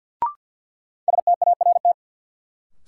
Morse code, heard as a keyed tone, about 47 wpm, 700 Hz. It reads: START
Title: Sets of 4 Words - New Word Drill - 50wpm